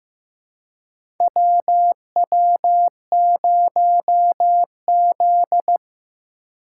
Morse code, heard as a keyed tone, about 15 wpm, 700 Hz. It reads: WW0Z
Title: USA Call Signs - 2x Character Spacing - 15wpm